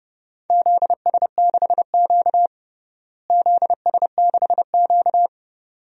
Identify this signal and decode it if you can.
Morse code, heard as a keyed tone, about 30 wpm, 700 Hz. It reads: ZS6Q ZS6Q
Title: DXCC-100 Call Signs - 4 Characters - Repeated 2x - 30wpm